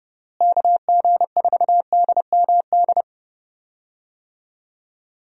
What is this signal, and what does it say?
Morse code, heard as a keyed tone, about 30 wpm, 700 Hz. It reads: KG4DMD
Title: USA Call Signs - 6 Characters - Repeated 3x - 30wpm